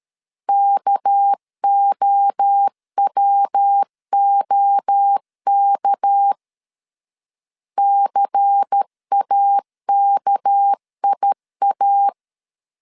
Morse code, heard as a keyed tone, about 13 wpm, 800 Hz. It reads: KOWOK CAKIA
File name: CW / DET / 1 6 1500C S B 12wpm 3431